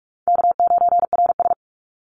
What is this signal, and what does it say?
Morse code, heard as a keyed tone, about 44 wpm, 700 Hz. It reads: K9RS
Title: Easy Call Signs Repeated 2x - 45wpm